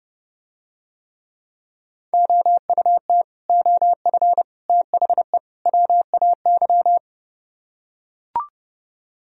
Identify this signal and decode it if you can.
Morse code, heard as a keyed tone, about 29 wpm, 700 Hz. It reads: OUT OF THE WAY